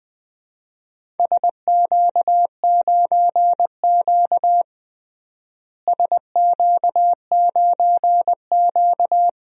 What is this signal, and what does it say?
Morse code, heard as a keyed tone, about 20 wpm, 700 Hz. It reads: SQ9Q SQ9Q
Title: DXCC-100 Call Signs - 4 Characters - Repeated 3x - 20wpm